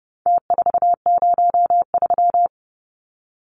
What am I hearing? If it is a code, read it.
Morse code, heard as a keyed tone, about 30 wpm, 700 Hz. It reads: T403
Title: Numbers - Contest Style - 4-Digit 30wpm